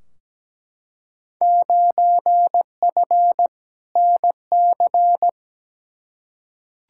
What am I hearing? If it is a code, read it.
Morse code, heard as a keyed tone, about 17 wpm, 700 Hz. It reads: 9F NC
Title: Field Day Exchanges Encore - 17wpm